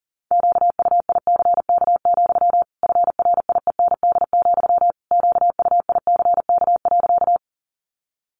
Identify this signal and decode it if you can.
Morse code, heard as a keyed tone, about 40 wpm, 700 Hz. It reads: QUICK, FRIEND, QUICK.
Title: Sentences from Top 500 Words 40wpm